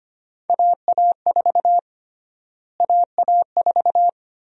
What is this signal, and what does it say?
Morse code, heard as a keyed tone, about 25 wpm, 700 Hz. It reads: AA4 AA4
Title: USA Call Sign Prefixes - Repeated 2x - 25wpm